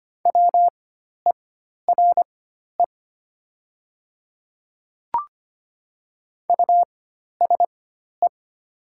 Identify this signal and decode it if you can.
Morse code, heard as a keyed tone, about 25 wpm, spaced as for 12 wpm, 700 Hz. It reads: WERE USE
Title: 4x Character Spacing - Top 100 Words - 25wpm